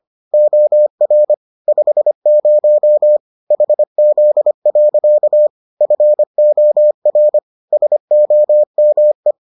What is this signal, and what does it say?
Morse code, heard as a keyed tone, about 25 wpm, 600 Hz. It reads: OR 50 HZ. FOR SOME